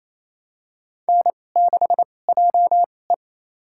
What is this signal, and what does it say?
Morse code, heard as a keyed tone, about 28 wpm, spaced as for 20 wpm, 700 Hz. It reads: N6JE
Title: USA Call Signs - 4 Characters - 2x Character Spacing - 28wpm